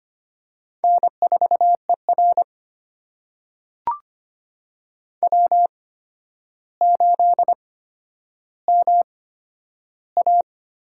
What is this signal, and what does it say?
Morse code, heard as a keyed tone, about 25 wpm, spaced as for 11 wpm, 700 Hz. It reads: N4ER W8MA
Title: USA Call Signs - 8x Character Spacing - 25wpm